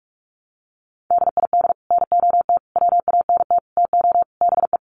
Morse code, heard as a keyed tone, about 44 wpm, 700 Hz. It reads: DID NOT WANT TO BE
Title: Sets of 5 Words 45wpm